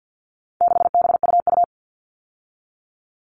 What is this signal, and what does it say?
Morse code, heard as a keyed tone, about 50 wpm, 700 Hz. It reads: 6BUU
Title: All Cut Numbers - 4-Digit 50wpm